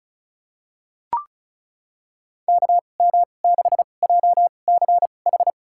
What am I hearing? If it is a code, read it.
Morse code, heard as a keyed tone, about 34 wpm, spaced as for 25 wpm, 700 Hz. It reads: KM6JCH
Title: USA Call Signs - 2x Character Spacing - 35wpm